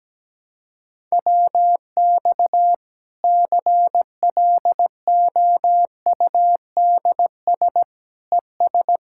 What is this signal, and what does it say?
Morse code, heard as a keyed tone, about 17 wpm, 700 Hz. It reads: WX CLOUDS ES
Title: K1USN SST QSO Practice 17wpm